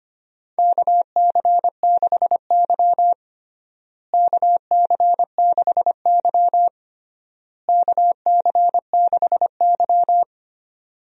Morse code, heard as a keyed tone, about 25 wpm, 700 Hz. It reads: KC6Y KC6Y KC6Y
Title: USA Call Signs - 4 Characters - Repeated 3x - 25wpm